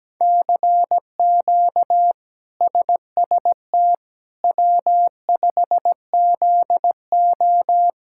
Morse code, heard as a keyed tone, about 17 wpm, 700 Hz. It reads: CQ SST W5ZO
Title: K1USN SST QSO Practice - 17wpm